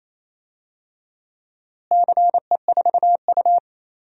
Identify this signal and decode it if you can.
Morse code, heard as a keyed tone, about 27 wpm, 700 Hz. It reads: CE4U
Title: DXCC-100 Call Signs - 4 Characters - Repeated 2x - 28wpm